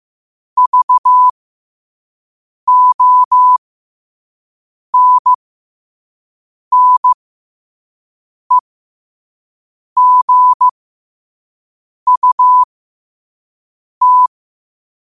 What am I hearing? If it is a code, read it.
Morse code, heard as a keyed tone, about 15 wpm, spaced as for 5 wpm, 1000 Hz. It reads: VONNEGUT
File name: qotdfiles / 5wpm